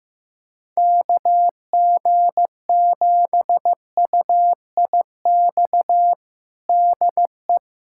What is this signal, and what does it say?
Morse code, heard as a keyed tone, about 15 wpm, 700 Hz. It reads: KG7UIX DE
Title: USA Callsign DE Callsign - 6 Characters - 15wpm